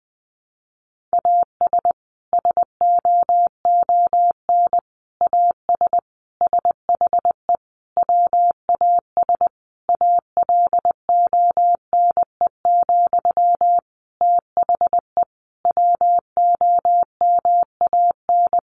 Morse code, heard as a keyed tone, about 20 wpm, 700 Hz. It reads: AS SOON AS SHE WAS ALONE, THE WOMAN
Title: Sentences from Top 1100 Words 20wpm